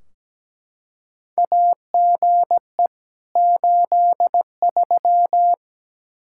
Morse code, heard as a keyed tone, about 17 wpm, 700 Hz. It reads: AGE 83
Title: Practice QSO - Exchange 3 - More Info - 17wpm